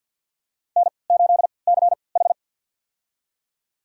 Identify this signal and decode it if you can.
Morse code, heard as a keyed tone, about 53 wpm, spaced as for 29 wpm, 700 Hz. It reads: N8CH
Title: USA Call Signs - 4 Characters - 3x Character Spacing - 50wpm